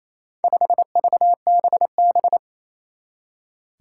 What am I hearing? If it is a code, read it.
Morse code, heard as a keyed tone, about 28 wpm, 700 Hz. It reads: 5VBB
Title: Four Challenging Letter-Number Combinations 28wpm